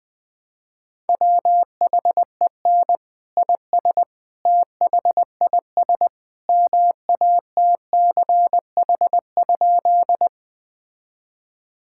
Morse code, heard as a keyed tone, about 20 wpm, 700 Hz. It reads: WHEN IS THIS MATCH?
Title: Sentences from Top 1600 Words 20wpm